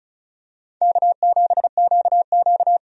Morse code, heard as a keyed tone, about 34 wpm, 700 Hz. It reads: K7QQ
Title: USA Call Signs - 4 Characters - 35wpm